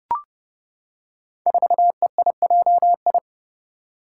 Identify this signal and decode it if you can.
Morse code, heard as a keyed tone, about 30 wpm, 700 Hz. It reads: 4EIJI
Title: Five Challenging Letter-Number Combinations 30wpm